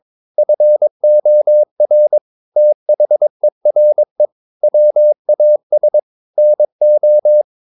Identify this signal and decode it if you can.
Morse code, heard as a keyed tone, about 22 wpm, 600 Hz. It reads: FOR THERE WAS NO
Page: Romanen Never af Ken Follet på engelsk.